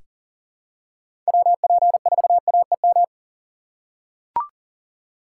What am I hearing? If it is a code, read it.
Morse code, heard as a keyed tone, about 40 wpm, 700 Hz. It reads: WP4AEM
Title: USA Call Signs - 6 Characters - 2x Character Spacing - 40wpm